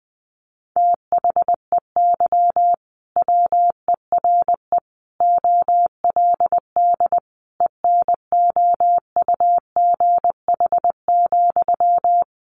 Morse code, heard as a keyed tone, about 20 wpm, 700 Hz. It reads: THEY WERE OLD ENOUGH,
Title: Sentences from Top 800 Words 20wpm